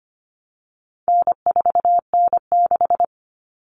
Morse code, heard as a keyed tone, about 25 wpm, 700 Hz. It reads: N4N6
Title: Common Cut Numbers - 4-Digit 25wpm